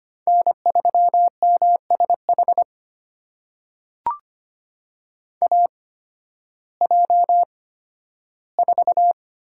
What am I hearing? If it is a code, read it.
Morse code, heard as a keyed tone, about 25 wpm, spaced as for 13 wpm, 700 Hz. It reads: N3MSH AJ4
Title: USA Call Signs - 5 Characters - 8x Character Spacing - 25wpm